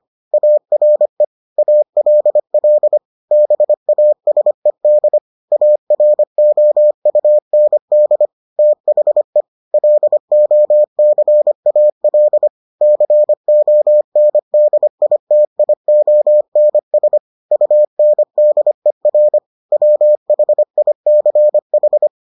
Morse code, heard as a keyed tone, about 25 wpm, 600 Hz. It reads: ARE ALL BASED AROUND THE LOCAL CONDITIONS UNDER WHICH